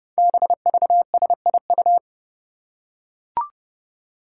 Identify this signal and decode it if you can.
Morse code, heard as a keyed tone, about 30 wpm, 700 Hz. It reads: BVSIU
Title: Five Challenging Letter-Number Combinations 30wpm